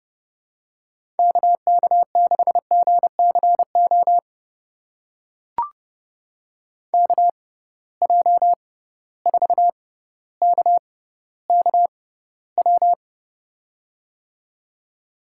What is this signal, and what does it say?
Morse code, heard as a keyed tone, about 32 wpm, spaced as for 16 wpm, 700 Hz. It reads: KK6GCO KJ4KKW
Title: USA Call Signs - 6 Characters - 6x Character Spacing - 30wpm